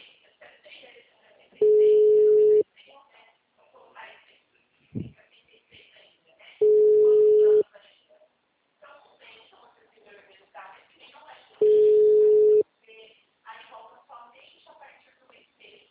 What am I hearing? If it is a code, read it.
Ringback tone (Europe, 425 Hz), with speech.